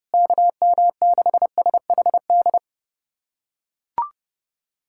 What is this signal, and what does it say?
Morse code, heard as a keyed tone, about 30 wpm, 700 Hz. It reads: KM6SHD
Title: USA Callsign DE Callsign 30wpm